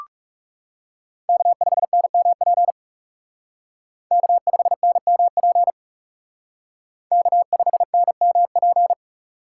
Morse code, heard as a keyed tone, about 36 wpm, 700 Hz. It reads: K5NMP K5NMP K5NMP
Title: Speed Racing - USA 1x3 Call Signs 30wpm